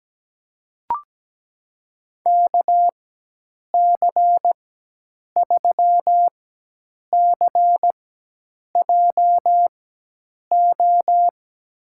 Morse code, heard as a keyed tone, about 17 wpm, spaced as for 8 wpm, 700 Hz. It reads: KC3CJO
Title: USA Call Signs - 6 Characters - 4x Character Spacing - 17wpm